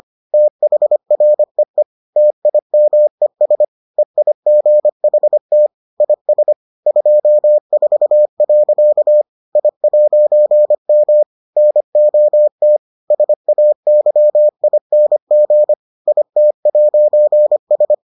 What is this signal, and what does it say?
Morse code, heard as a keyed tone, about 25 wpm, 600 Hz. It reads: THREE TIMES EIGHT IS 24. I'M NOT SAYING IT'S